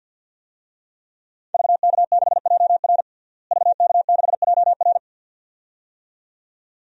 Morse code, heard as a keyed tone, about 48 wpm, 700 Hz. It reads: VK6JR VK6JR
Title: DXCC-100 Call Signs - 5 Characters - Repeated 2x - 50wpm